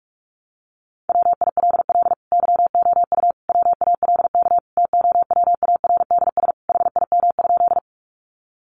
Morse code, heard as a keyed tone, about 44 wpm, 700 Hz. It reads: WILL YOU WALK TOWARDS HIM?
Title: Sentences from Top 1100 Words 45wpm